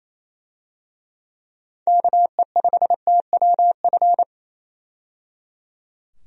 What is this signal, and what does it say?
Morse code, heard as a keyed tone, about 28 wpm, 700 Hz. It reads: KE5TWF
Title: USA Call Signs - 6 Characters - 28wpm